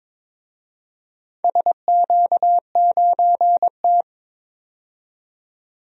Morse code, heard as a keyed tone, about 22 wpm, 700 Hz. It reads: SQ9T
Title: DXCC-100 Call Signs - 4 Characters - 22wpm